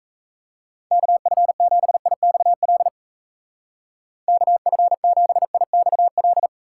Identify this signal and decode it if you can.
Morse code, heard as a keyed tone, about 37 wpm, 700 Hz. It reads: KF7IXL KF7IXL